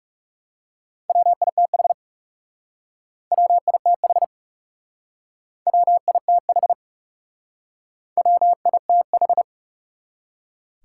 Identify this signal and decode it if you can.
Morse code, heard as a keyed tone, about 40 wpm, 700 Hz. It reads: WITH WITH WITH WITH